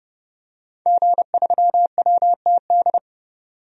Morse code, heard as a keyed tone, about 30 wpm, 700 Hz. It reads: G3WTD
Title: DXCC-100 Call Signs - 5 Characters - 30wpm